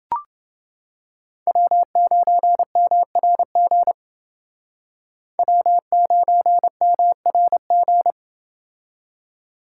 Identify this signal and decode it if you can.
Morse code, heard as a keyed tone, about 30 wpm, 700 Hz. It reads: W9MRG W9MRG